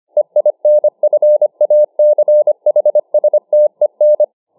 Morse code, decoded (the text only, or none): EINFACHSTEN